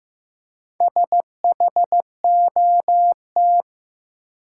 Morse code, heard as a keyed tone, about 15 wpm, 700 Hz. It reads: SHOT
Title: Four Letter Words - 15wpm